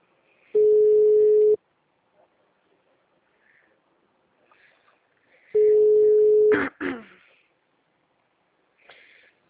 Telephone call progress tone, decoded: ringback tone

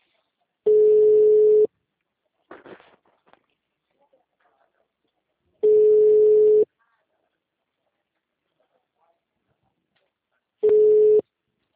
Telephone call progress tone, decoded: ringback tone

